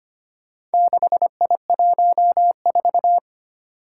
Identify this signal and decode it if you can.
Morse code, heard as a keyed tone, about 25 wpm, 700 Hz. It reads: 6I14
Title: Four Challenging Letter-Number Combinations 25wpm